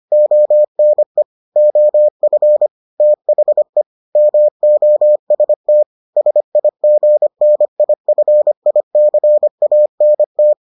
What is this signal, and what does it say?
Morse code, heard as a keyed tone, about 25 wpm, 600 Hz. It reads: ONE OF THE MOST SIGNIFICANT